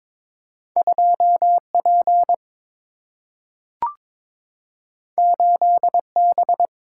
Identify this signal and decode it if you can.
Morse code, heard as a keyed tone, about 22 wpm, 700 Hz. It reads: 2 P 8 B
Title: Single-Letter Single-Number 22wpm